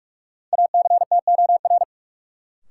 Morse code, heard as a keyed tone, about 41 wpm, 700 Hz. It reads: ACTOR
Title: Sentences from Top 1500 Words - New Word Drill Alt - 45wpm